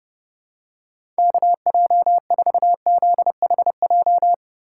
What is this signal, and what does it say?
Morse code, heard as a keyed tone, about 30 wpm, 700 Hz. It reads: KJ4ZHJ
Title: USA Call Signs - 6 Characters - 3x Character Spacing - 30wpm